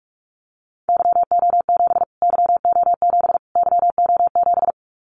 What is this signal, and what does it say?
Morse code, heard as a keyed tone, about 44 wpm, 700 Hz. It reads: YO7 YO7 YO7
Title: Callsign Prefixes for Top 25 DXCC - Repeated 3x - 45wpm